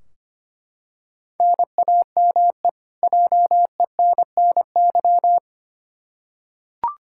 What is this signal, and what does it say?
Morse code, heard as a keyed tone, about 25 wpm, 700 Hz. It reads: NAME JENNY
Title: K1USN SST QSO Practice 25wpm